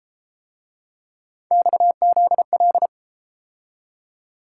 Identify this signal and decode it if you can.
Morse code, heard as a keyed tone, about 32 wpm, 700 Hz. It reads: XZL